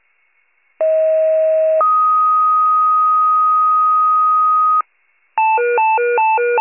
Two-tone page, heard as a one-second tone, then a three-second tone, about 640 then 1250 Hz. It is heard from 0.8 to 4.8 s.